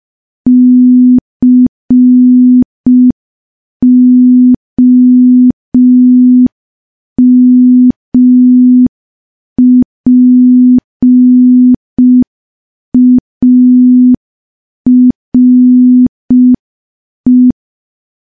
Morse code, heard as a keyed tone, about 5 wpm, 250 Hz. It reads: COMPARE
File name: morse / morse-training-05WPM